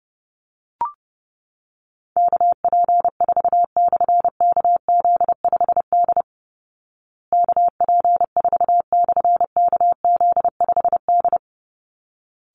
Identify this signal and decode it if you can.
Morse code, heard as a keyed tone, about 30 wpm, 700 Hz. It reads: KP4/KZ5D KP4/KZ5D
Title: Call Signs Repeated 3x - 30wpm